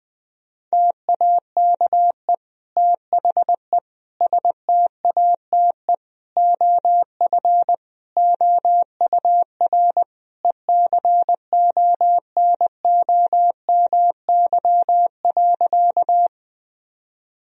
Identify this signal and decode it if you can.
Morse code, heard as a keyed tone, about 20 wpm, 700 Hz. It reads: TAKE THE STATE OF OUR ECONOMY.
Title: Sentences from Top 1000 Words - Review Part 3 - 20wpm